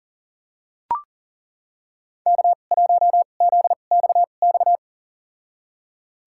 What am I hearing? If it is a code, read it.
Morse code, heard as a keyed tone, about 40 wpm, spaced as for 29 wpm, 700 Hz. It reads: K1ZXX
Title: USA Call Signs - 5 Characters - 2x Character Spacing - 40wpm